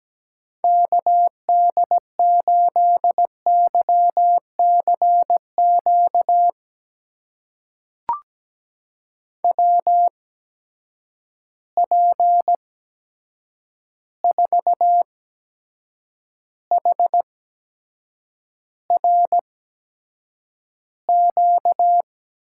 Morse code, heard as a keyed tone, about 17 wpm, spaced as for 7 wpm, 700 Hz. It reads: KD8YCQ WP4HRQ
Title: USA Call Signs - 6 Characters - 8x Character Spacing - 17wpm